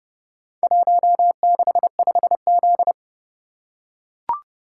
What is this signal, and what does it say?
Morse code, heard as a keyed tone, about 28 wpm, 700 Hz. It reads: 165Z